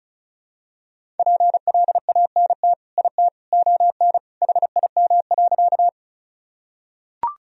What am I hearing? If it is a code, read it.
Morse code, heard as a keyed tone, about 34 wpm, 700 Hz. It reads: PLANT IT ON HIM.
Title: Sentences from Top 1000 Words - Review Part 1 - 35wpm